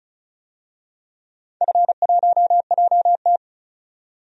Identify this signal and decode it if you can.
Morse code, heard as a keyed tone, about 36 wpm, 700 Hz. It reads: F1JT
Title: DXCC-100 Call Signs - 4 Characters - 35wpm